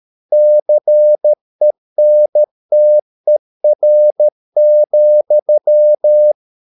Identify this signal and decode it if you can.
Morse code, heard as a keyed tone, about 13 wpm, 600 Hz. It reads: CENTER,